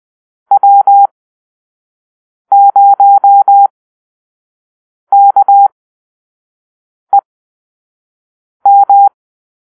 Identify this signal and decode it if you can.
Morse code, heard as a keyed tone, about 20 wpm, spaced as for 5 wpm, 800 Hz. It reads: W0KEM
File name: Call-Sign Drill